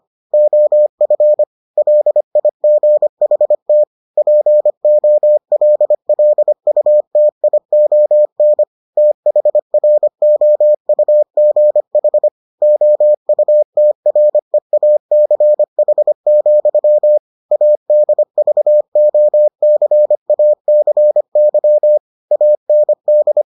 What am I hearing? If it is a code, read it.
Morse code, heard as a keyed tone, about 25 wpm, 600 Hz. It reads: OF LIGHT POLLUTION THROUGH OUTREACH, ADVOCACY AND